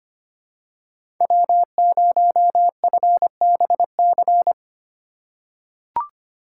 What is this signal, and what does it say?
Morse code, heard as a keyed tone, about 25 wpm, 700 Hz. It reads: W0FBC
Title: USA Call Signs - 4x Character Spacing - 25wpm